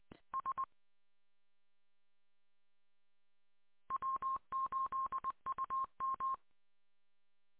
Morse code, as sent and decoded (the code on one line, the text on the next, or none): ... / .-- ---.. ..- --
S W8UM